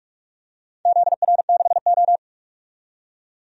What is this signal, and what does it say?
Morse code, heard as a keyed tone, about 46 wpm, 700 Hz. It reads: ZR6O
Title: DXCC-100 Call Signs - 4 Characters - 45wpm